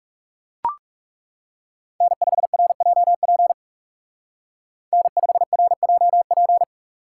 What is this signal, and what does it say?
Morse code, heard as a keyed tone, about 35 wpm, 700 Hz. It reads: N5RJP N5RJP
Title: Speed Racing - USA 1x3 Call Signs 30wpm